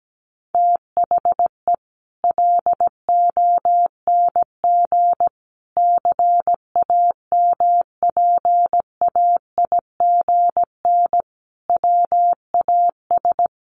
Morse code, heard as a keyed tone, about 17 wpm, 700 Hz. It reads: THE LONG CAMPAIGN WAS
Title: Sentences from Top 1200 Words 17wpm